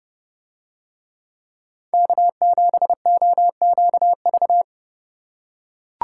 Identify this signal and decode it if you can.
Morse code, heard as a keyed tone, about 28 wpm, 700 Hz. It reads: K7OQV